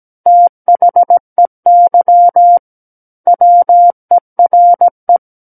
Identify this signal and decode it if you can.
Morse code, heard as a keyed tone, about 17 wpm, 700 Hz. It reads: THEY WERE
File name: Book One / 08-Chapter 8 - 17 WPM